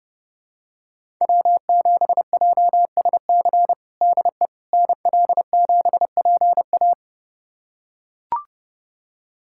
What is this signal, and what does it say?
Morse code, heard as a keyed tone, about 30 wpm, 700 Hz. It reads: W7JSC DE NL7PA
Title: USA Callsign DE Callsign - 5 Characters - 30wpm